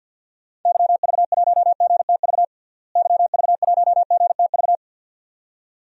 Morse code, heard as a keyed tone, about 53 wpm, 700 Hz. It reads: YV1GTV YV1GTV
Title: DXCC-100 Call Signs - 6 Characters - Repeated 2x - 50wpm